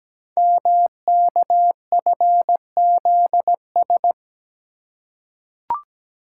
Morse code, heard as a keyed tone, about 17 wpm, 700 Hz. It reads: MKFZS
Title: Code Groups - 5 Letters - 17wpm